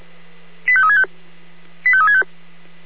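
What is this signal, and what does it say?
Five-tone selective call (ZVEI), heard as quick runs of five tones, tones of about 70 ms. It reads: 85365 85365
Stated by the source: ZVEI five-tone call received by SDR.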